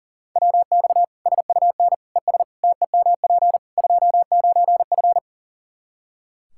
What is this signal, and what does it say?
Morse code, heard as a keyed tone, about 40 wpm, 700 Hz. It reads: WX SUN ES TEMP 29F
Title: K1USN SST QSO Practice 40wpm